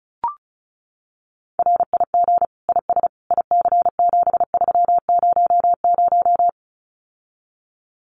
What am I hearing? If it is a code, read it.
Morse code, heard as a keyed tone, about 34 wpm, 700 Hz. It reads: RIG IS IC7300
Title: Common QSO Phrases Encore 35wpm